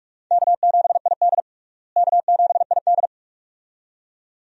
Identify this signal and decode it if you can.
Morse code, heard as a keyed tone, about 46 wpm, 700 Hz. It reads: K7ID K7ID
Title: USA Call Signs - 4 Characters - Repeated 2x - 45wpm